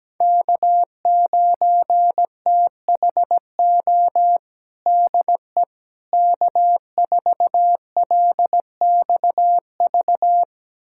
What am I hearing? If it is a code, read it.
Morse code, heard as a keyed tone, about 17 wpm, 700 Hz. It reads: K9THO DE K4LXV
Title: USA Callsign DE Callsign 17wpm